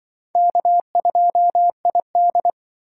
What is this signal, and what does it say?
Morse code, heard as a keyed tone, about 25 wpm, 690 Hz. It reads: K2ID
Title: Speed Racing - USA 1x2 Call Signs 20wpm